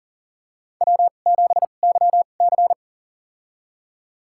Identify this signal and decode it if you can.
Morse code, heard as a keyed tone, about 40 wpm, spaced as for 29 wpm, 700 Hz. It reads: W7YC
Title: USA Call Signs - 2x Character Spacing - 40wpm